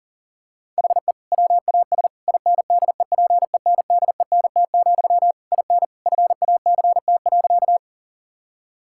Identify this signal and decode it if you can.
Morse code, heard as a keyed tone, about 40 wpm, 700 Hz. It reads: HE WAS INDEPENDENT, IN FACT.
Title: Sentences from Top 1000 Words - Review Part 6 - 40wpm